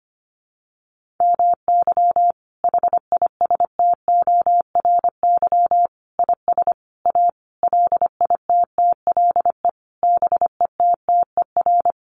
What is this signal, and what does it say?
Morse code, heard as a keyed tone, about 25 wpm, 700 Hz. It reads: MY HISTORY IS A LITTLE BETTER
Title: Sentences from Top 1500 Words 25wpm